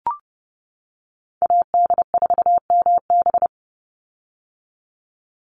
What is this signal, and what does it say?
Morse code, heard as a keyed tone, about 30 wpm, 700 Hz. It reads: AD4MB
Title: USA Call Signs 30wpm